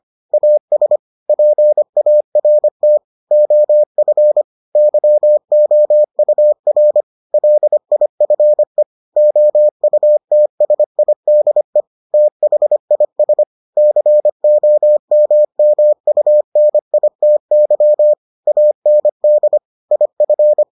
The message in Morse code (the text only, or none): AS PART OF YOUR LIFE OUTSIDE THIS COMMUNITY AND IF